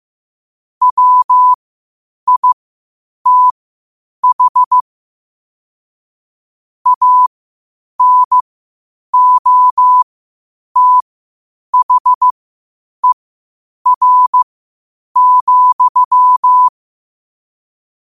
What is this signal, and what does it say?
Morse code, heard as a keyed tone, about 15 wpm, spaced as for 8 wpm, 1000 Hz. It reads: WITH ANOTHER,
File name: 8wpm